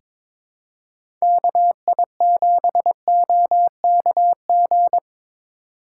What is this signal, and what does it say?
Morse code, heard as a keyed tone, about 22 wpm, 700 Hz. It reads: KI7OKG
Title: USA Call Signs - 6 Characters - 4x Character Spacing - 22wpm